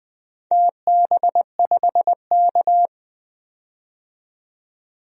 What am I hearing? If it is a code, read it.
Morse code, heard as a keyed tone, about 20 wpm, 700 Hz. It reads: TB5K
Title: DXCC-100 Call Signs - 4 Characters - 20wpm